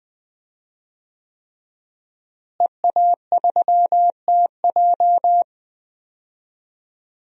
Morse code, heard as a keyed tone, about 21 wpm, 700 Hz. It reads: EA3TJ
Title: DXCC-100 Call Signs - 5 Characters - Repeated 3x - 20wpm